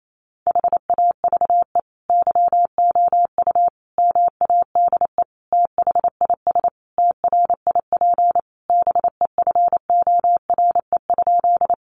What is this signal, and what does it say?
Morse code, heard as a keyed tone, about 28 wpm, 700 Hz. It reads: HAVE YOU MADE THIS TRIP BEFORE?
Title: Sentences from Top 1300 Words 28wpm